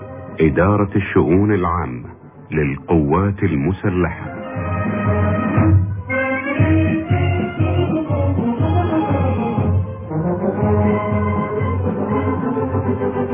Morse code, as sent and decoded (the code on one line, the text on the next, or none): none